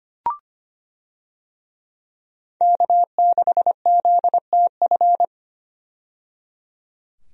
Morse code, heard as a keyed tone, about 25 wpm, 700 Hz. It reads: K6ZTF
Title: USA Call Signs - 5 Characters - 25wpm